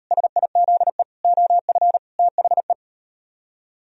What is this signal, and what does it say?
Morse code, heard as a keyed tone, about 38 wpm, 700 Hz. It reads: SIZE OF THE